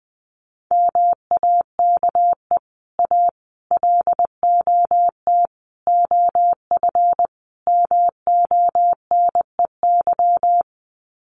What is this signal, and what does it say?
Morse code, heard as a keyed tone, about 20 wpm, 700 Hz. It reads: MAKE A LOT OF MONEY
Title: Sets of 5 Words Encore2 20wpm